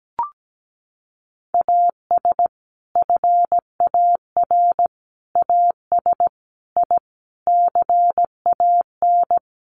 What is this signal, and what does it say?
Morse code, heard as a keyed tone, about 17 wpm, 700 Hz. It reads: AS FAR AS I CAN
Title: Sets of 5 Words Encore 17wpm